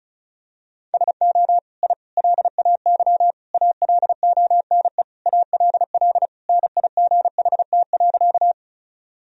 Morse code, heard as a keyed tone, about 36 wpm, 700 Hz. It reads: SO I LAY ALONE ALL NIGHT.
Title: Sentences from Top 1000 Words - Review Part 2 - 35wpm